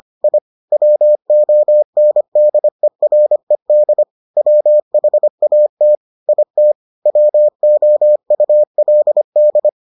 Morse code, heard as a keyed tone, about 25 wpm, 600 Hz. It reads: I WONDERED WHAT IT WOULD